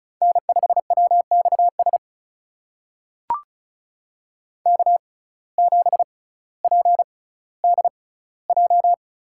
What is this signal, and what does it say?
Morse code, heard as a keyed tone, about 36 wpm, spaced as for 18 wpm, 700 Hz. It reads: N5WXS K7PDJ
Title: USA Call Signs - 5 Characters - 6x Character Spacing - 35wpm